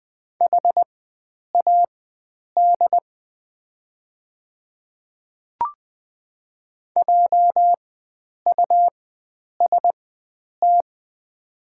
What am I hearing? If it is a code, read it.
Morse code, heard as a keyed tone, about 20 wpm, spaced as for 9 wpm, 700 Hz. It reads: HAD JUST
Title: 4x Character Spacing - Top 300 Words - 20wpm